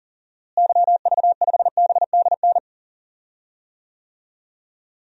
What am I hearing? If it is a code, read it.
Morse code, heard as a keyed tone, about 40 wpm, 700 Hz. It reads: YV5BDN
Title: DXCC-100 Call Signs - 6 Characters - Repeated 2x - 40wpm